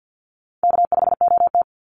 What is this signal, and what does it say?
Morse code, heard as a keyed tone, about 50 wpm, 700 Hz. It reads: K5OT
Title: Easy Call Signs - 50wpm